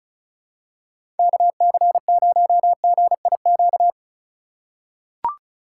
Morse code, heard as a keyed tone, about 36 wpm, 700 Hz. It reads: KC0GIQ
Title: USA Call Signs - 6 Characters - 3x Character Spacing - 35wpm